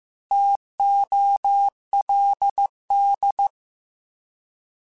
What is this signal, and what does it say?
Morse code, heard as a keyed tone, about 15 wpm, 800 Hz. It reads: TOLD
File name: T Prefixes / T mp3s